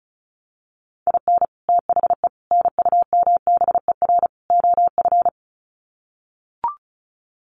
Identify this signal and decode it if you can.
Morse code, heard as a keyed tone, about 34 wpm, 700 Hz. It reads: IN THE NUMBER OF
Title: Sets of 4 Words 35wpm